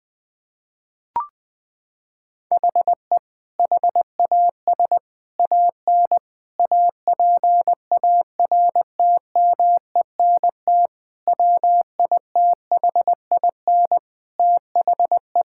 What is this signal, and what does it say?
Morse code, heard as a keyed tone, about 21 wpm, 700 Hz. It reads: HE HAS AN APARTMENT WITHIN THE
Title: Sentences from Top 1000 Words - Review Part 6 - 20wpm